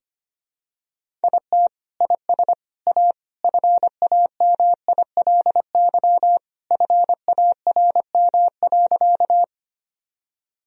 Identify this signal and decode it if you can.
Morse code, heard as a keyed tone, about 25 wpm, 700 Hz. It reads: IT IS A FAMILY FARM.
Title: Sentences from Top 1000 Words - Review Part 2 - 25wpm